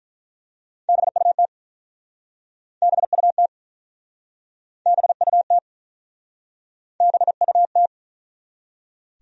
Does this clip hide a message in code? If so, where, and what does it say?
Yes: Morse code from 0.9 to 7.9 s, heard as a keyed tone, about 50 wpm, 700 Hz. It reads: BUT BUT BUT BUT